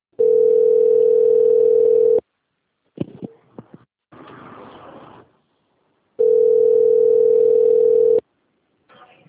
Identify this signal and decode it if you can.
Ringback tone (North America), 440 + 480 Hz.